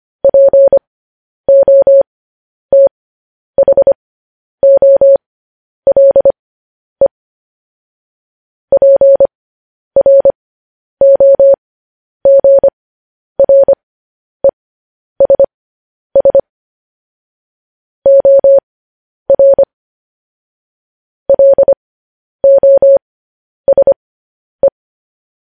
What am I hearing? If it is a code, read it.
Morse code, heard as a keyed tone, about 25 wpm, spaced as for 10 wpm, 550 Hz. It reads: POTHOLE PROGRESS OR LOSE
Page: news-headlines-in-morse-code-at-10-wpm